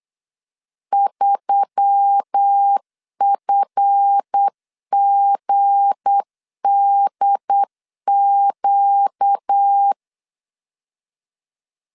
Morse code, heard as a keyed tone, about 8 wpm, 800 Hz. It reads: 3FGDQ
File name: CW / DET / 1 9 1500C S C 8wpm 5653